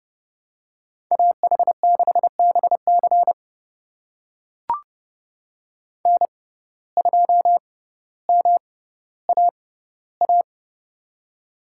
Morse code, heard as a keyed tone, about 30 wpm, spaced as for 15 wpm, 700 Hz. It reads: AH6BC N2MAA
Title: USA Call Signs - 5 Characters - 6x Character Spacing - 30wpm